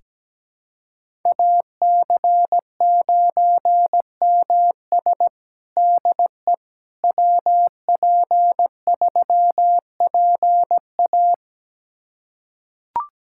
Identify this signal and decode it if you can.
Morse code, heard as a keyed tone, about 17 wpm, 700 Hz. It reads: AC9MS DE WP3PA
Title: USA Callsign DE Callsign - 5 Characters - 17wpm